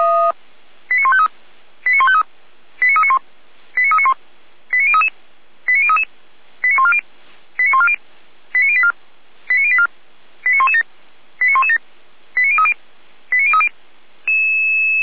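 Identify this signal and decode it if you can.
Five-tone selective call (ZVEI), heard as quick runs of five tones, tones of about 70 ms. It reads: 89153 89153 89391 89391 89033 89033 89140 89140 89084 89084 89117 89117 89030 89030
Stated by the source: ZVEI five-tone selective call received on an SDR.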